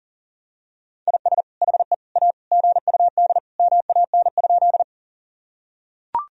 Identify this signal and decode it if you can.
Morse code, heard as a keyed tone, about 40 wpm, 700 Hz. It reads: IS HE A GUD MAN?
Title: Sentences from Top 200 Words and Abbreviations 40wpm